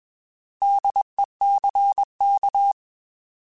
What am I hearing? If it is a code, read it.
Morse code, heard as a keyed tone, about 21 wpm, 800 Hz. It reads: DECK
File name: Alphabetical / Alphabetical mp3s / Ds @20